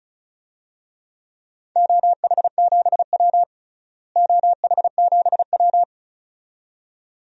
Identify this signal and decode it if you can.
Morse code, heard as a keyed tone, about 36 wpm, 700 Hz. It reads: OH7W OH7W
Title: DXCC-100 Call Signs - 4 Characters - Repeated 2x - 35wpm